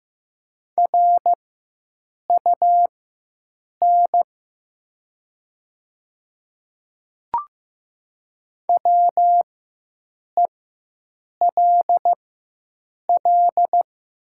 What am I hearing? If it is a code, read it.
Morse code, heard as a keyed tone, about 15 wpm, spaced as for 7 wpm, 700 Hz. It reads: RUN WELL
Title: 4x Character Spacing - Top 500 Words - 15wpm